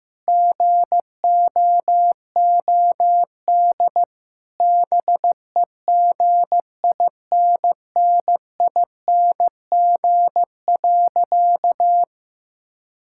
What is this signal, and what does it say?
Morse code, heard as a keyed tone, about 15 wpm, 700 Hz. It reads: GOOD BEGINNING.
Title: Sentences from Top 1000 Words - Review Part 5 - 15wpm